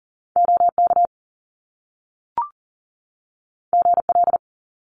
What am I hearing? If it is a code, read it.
Morse code, heard as a keyed tone, about 41 wpm, 700 Hz. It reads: OK GL
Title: Top Two Letter Combinations in Words 40wpm